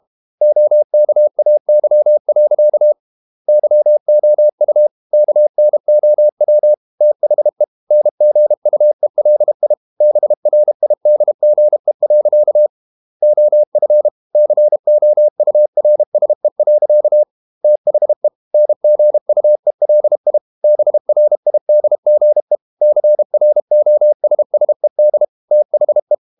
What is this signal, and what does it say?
Morse code, heard as a keyed tone, about 32 wpm, 600 Hz. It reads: OKAY. YOU KNOW THE NGUELI BRIDGE. OF COURSE. THE NGUELI BRIDGE CROSSED THE
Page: Never 31wpm | CW med Gnister